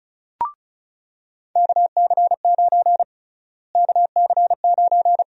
Morse code, heard as a keyed tone, about 36 wpm, 700 Hz. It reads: KC9 KC9
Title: USA Call Sign Prefixes - Repeated 2x - 35wpm